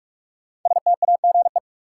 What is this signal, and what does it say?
Morse code, heard as a keyed tone, about 46 wpm, 700 Hz. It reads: STAGE
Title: Sentences from Top 1200 Words - New Word Drill Alt - 45wpm